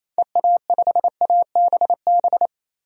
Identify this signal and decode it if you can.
Morse code, heard as a keyed tone, about 28 wpm, 700 Hz. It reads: EA5ABB
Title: DXCC-100 Call Signs - 6 Characters - 28wpm